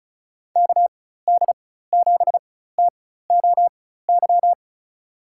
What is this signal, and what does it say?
Morse code, heard as a keyed tone, about 36 wpm, spaced as for 16 wpm, 700 Hz. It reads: KD7TOY
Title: USA Call Signs - 6 Characters - 4x Character Spacing - 35wpm